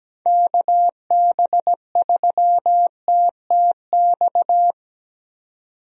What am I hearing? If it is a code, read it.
Morse code, heard as a keyed tone, about 17 wpm, 700 Hz. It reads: KB3TTX
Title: USA Call Signs - 6 Characters - 17wpm